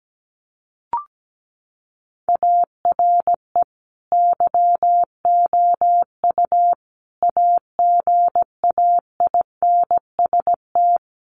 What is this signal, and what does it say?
Morse code, heard as a keyed tone, about 17 wpm, 700 Hz. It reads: ARE YOU AGAINST
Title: Sentences from Top 1400 Words 17wpm